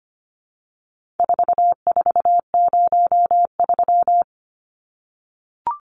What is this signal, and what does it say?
Morse code, heard as a keyed tone, about 25 wpm, 700 Hz. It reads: 4403
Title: Numbers - 4-Digit 25wpm